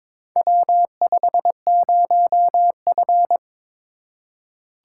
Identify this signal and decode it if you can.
Morse code, heard as a keyed tone, about 22 wpm, 700 Hz. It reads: W50F
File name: practice / 22wpm / Four Letter-Number Combinations - 22wpm